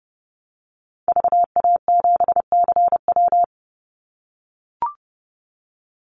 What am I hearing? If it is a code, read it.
Morse code, heard as a keyed tone, about 30 wpm, 700 Hz. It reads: VA7CW
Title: Call Signs - 30wpm